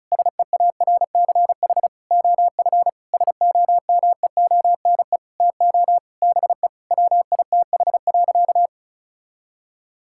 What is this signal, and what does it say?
Morse code, heard as a keyed tone, about 36 wpm, 700 Hz. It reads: SEARCH OF SOMEONE TO BE WITH.
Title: Sentences from Top 1000 Words - Review Part 5 - 35wpm